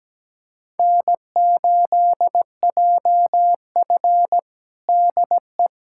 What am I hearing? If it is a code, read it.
Morse code, heard as a keyed tone, about 17 wpm, 700 Hz. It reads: N8JF DE
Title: USA Callsign DE Callsign - 4 Characters - 17wpm